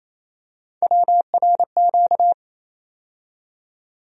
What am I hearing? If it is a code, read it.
Morse code, heard as a keyed tone, about 28 wpm, 700 Hz. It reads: WRQ
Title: USA Call Sign Postfixes - 3 Letters 28wpm